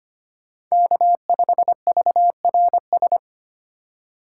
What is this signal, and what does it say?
Morse code, heard as a keyed tone, about 25 wpm, 700 Hz. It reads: K5VRS
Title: USA Call Signs - 5 Characters - 2x Character Spacing - 25wpm